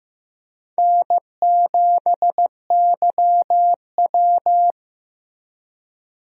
Morse code, heard as a keyed tone, about 15 wpm, 700 Hz. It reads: N7YW
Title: USA Call Signs - 4 Characters - Repeated 3x - 15wpm